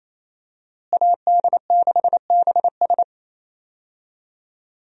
Morse code, heard as a keyed tone, about 28 wpm, 700 Hz. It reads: AD6BS
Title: USA Call Signs - 5 Characters - Repeated 3x - 28wpm